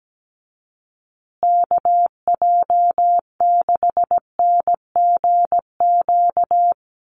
Morse code, heard as a keyed tone, about 17 wpm, 700 Hz. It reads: KJ6NGQ
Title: USA Call Signs Repeated 2x 17wpm